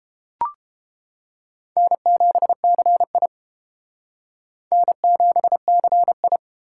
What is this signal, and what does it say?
Morse code, heard as a keyed tone, about 31 wpm, 700 Hz. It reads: N7CI N7CI